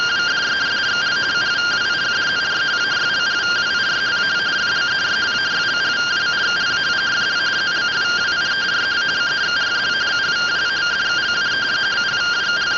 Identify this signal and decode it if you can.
Two-tone teleprinter signal, about 170 Hz shift.